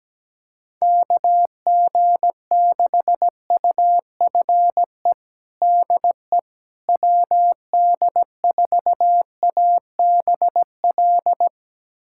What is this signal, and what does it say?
Morse code, heard as a keyed tone, about 17 wpm, 700 Hz. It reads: KG6UFE DE WD4ABL
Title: USA Callsign DE Callsign 17wpm